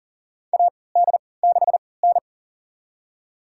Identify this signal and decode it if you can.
Morse code, heard as a keyed tone, about 40 wpm, spaced as for 23 wpm, 700 Hz. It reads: AD6N
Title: USA Call Signs - 4 Characters - 3x Character Spacing - 40wpm